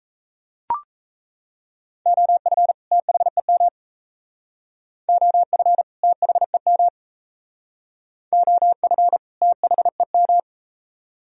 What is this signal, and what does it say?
Morse code, heard as a keyed tone, about 39 wpm, 700 Hz. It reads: OF THEM OF THEM OF THEM